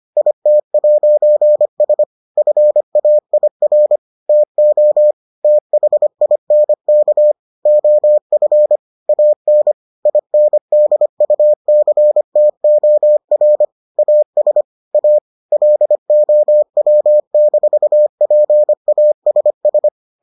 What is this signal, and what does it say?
Morse code, heard as a keyed tone, about 25 wpm, 600 Hz. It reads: IT'S FAIR TO THINK OF AN INDUCTOR AS A LOW-PASS